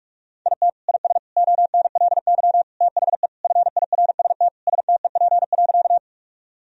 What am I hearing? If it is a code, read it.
Morse code, heard as a keyed tone, about 41 wpm, 700 Hz. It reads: IT IS ONLY THE FIRST STEP.
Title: Sentences from Top 1000 Words - Review Part 2 - 45wpm